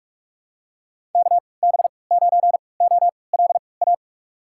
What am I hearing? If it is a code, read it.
Morse code, heard as a keyed tone, about 46 wpm, spaced as for 25 wpm, 690 Hz. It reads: KB9OLA
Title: USA Call Signs - 6 Characters - 3x Character Spacing - 45wpm